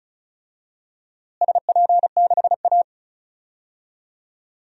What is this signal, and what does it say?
Morse code, heard as a keyed tone, about 34 wpm, 700 Hz. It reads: SP6A
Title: DXCC-100 Call Signs - 4 Characters - Repeated 2x - 35wpm